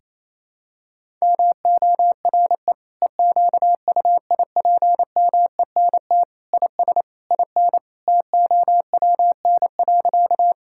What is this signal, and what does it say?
Morse code, heard as a keyed tone, about 27 wpm, 700 Hz. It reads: MORE EQUIPMENT IS IN TOWN.
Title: Sentences from Top 1000 Words - Review Part 5 - 28wpm